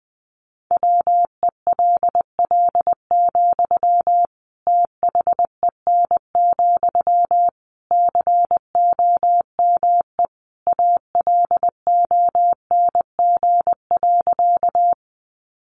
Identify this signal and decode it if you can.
Morse code, heard as a keyed tone, about 20 wpm, 700 Hz. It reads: WELL, THEN, COME ALONG.
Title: Sentences from Top 700 Words 20wpm